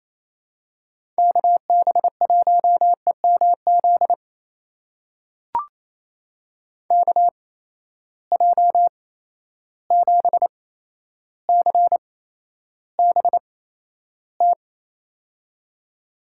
Morse code, heard as a keyed tone, about 27 wpm, spaced as for 12 wpm, 700 Hz. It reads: KB1EMZ KJ7CBT
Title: USA Call Signs - 6 Characters - 8x Character Spacing - 28wpm